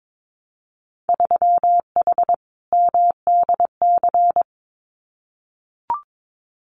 Morse code, heard as a keyed tone, about 22 wpm, 700 Hz. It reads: 3H MDC
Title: Winter Field Day Exchanges Encore 22wpm